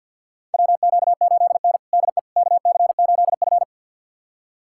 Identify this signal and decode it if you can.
Morse code, heard as a keyed tone, about 52 wpm, 700 Hz. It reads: WQ8N DE KC8F
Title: USA Callsign DE Callsign - 4 Characters - 50wpm